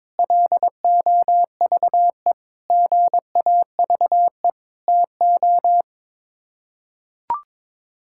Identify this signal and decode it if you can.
Morse code, heard as a keyed tone, about 22 wpm, 700 Hz. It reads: LOVE GAVE TO
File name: practice / 22wpm / Sets of 8 Words - 22wpm